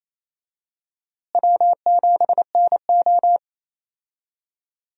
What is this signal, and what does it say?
Morse code, heard as a keyed tone, about 28 wpm, 700 Hz. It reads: W7NO
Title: USA Call Signs - 4 Characters - Repeated 2x - 28wpm